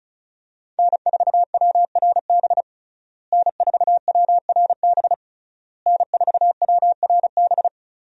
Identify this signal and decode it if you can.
Morse code, heard as a keyed tone, about 36 wpm, 700 Hz. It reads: N4WRB N4WRB N4WRB
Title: USA Call Signs - 5 Characters - Repeated 3x - 35wpm